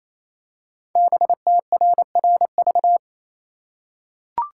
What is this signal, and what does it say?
Morse code, heard as a keyed tone, about 29 wpm, 700 Hz. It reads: BTRRV
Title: Code Groups - 5 Letters - 28wpm